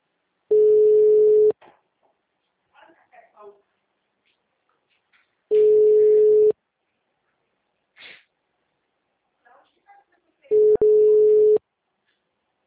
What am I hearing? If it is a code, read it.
Ringback tone (Europe, 425 Hz), with speech.
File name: out- -2046- - - .50095